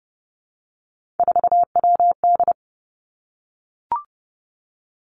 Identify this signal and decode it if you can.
Morse code, heard as a keyed tone, about 30 wpm, 700 Hz. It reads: 4WD
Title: Three Challenging Letter-Number Combinations 30wpm